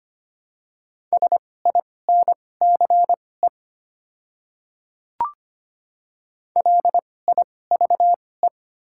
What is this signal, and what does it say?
Morse code, heard as a keyed tone, about 25 wpm, spaced as for 18 wpm, 700 Hz. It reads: SINCE LIVE
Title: 2x Character Spacing - Top 300 Words - 25wpm